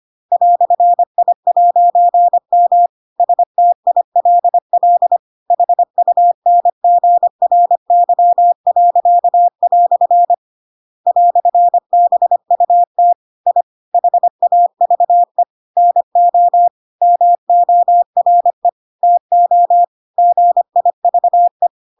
Morse code, heard as a keyed tone, about 25 wpm, 700 Hz. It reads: "I'M STILL HUNGRY." "BUT I HAVE NO MORE TO GIVE
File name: Pinocchio / Pinocchio ch7 25 25 ik1hge